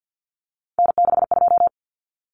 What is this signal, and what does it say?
Morse code, heard as a keyed tone, about 50 wpm, 700 Hz. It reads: N62
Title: Common Cut Numbers - 3-Digit 50wpm